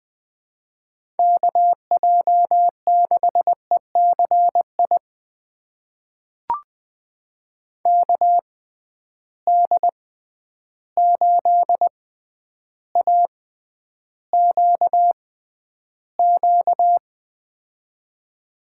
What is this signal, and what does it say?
Morse code, heard as a keyed tone, about 20 wpm, spaced as for 10 wpm, 700 Hz. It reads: KJ6ECI KD8AQQ
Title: USA Call Signs - 6 Characters - 6x Character Spacing - 20wpm